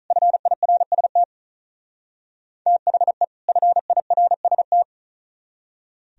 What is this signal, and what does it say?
Morse code, heard as a keyed tone, about 42 wpm, 700 Hz. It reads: FIRST THE FIRST
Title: Speed-Racing - Sets of 2 Words 35wpm